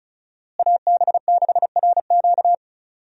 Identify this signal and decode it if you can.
Morse code, heard as a keyed tone, about 36 wpm, 700 Hz. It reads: AB6RQ
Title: USA Call Signs - 5 Characters - Repeated 3x - 35wpm